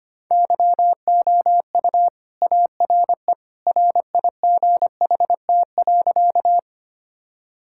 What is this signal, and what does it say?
Morse code, heard as a keyed tone, about 25 wpm, 690 Hz. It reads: YOU ARE RIGHT.
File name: practice / 25wpm / Sentences with 2-Word Phrases - 25wpm